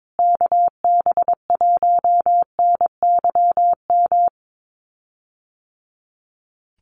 Morse code, heard as a keyed tone, about 22 wpm, 700 Hz. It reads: KB1NYM
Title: USA Call Signs Repeated 2x 22wpm